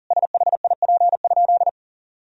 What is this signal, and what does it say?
Morse code, heard as a keyed tone, about 40 wpm, 700 Hz. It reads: SHIP?
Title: Sentences from Top 1000 Words - Review Part 2 - 40wpm